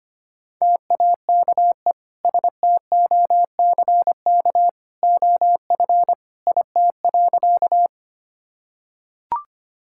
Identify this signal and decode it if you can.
Morse code, heard as a keyed tone, about 25 wpm, 700 Hz. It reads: TAKE STOCK OF IT.
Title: Sentences from Top 1000 Words - Review Part 5 - 25wpm